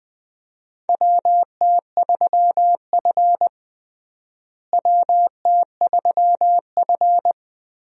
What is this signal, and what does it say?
Morse code, heard as a keyed tone, about 20 wpm, 700 Hz. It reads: WT3F WT3F
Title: USA Call Signs - 4 Characters - Repeated 3x - 20wpm